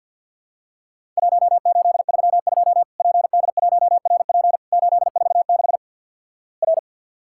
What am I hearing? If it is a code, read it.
Morse code, heard as a keyed tone, about 50 wpm, 700 Hz. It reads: 1832 PD1RP 846 R
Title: Worked All Europe DX Contest - QTC Exchange - 50wpm